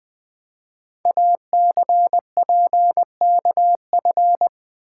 Morse code, heard as a keyed tone, about 20 wpm, 700 Hz. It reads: ACPKF
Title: Code Groups - 5 Letter:Numbers - 20wpm